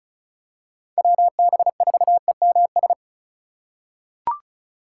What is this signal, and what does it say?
Morse code, heard as a keyed tone, about 34 wpm, 700 Hz. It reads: WB4EMS
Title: USA Call Signs - 2x Character Spacing - 35wpm